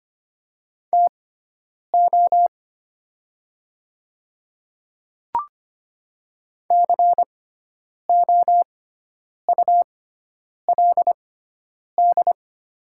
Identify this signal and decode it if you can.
Morse code, heard as a keyed tone, about 25 wpm, spaced as for 9 wpm, 700 Hz. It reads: TO COULD
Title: 6x Character Spacing - Top 100 Words - 25wpm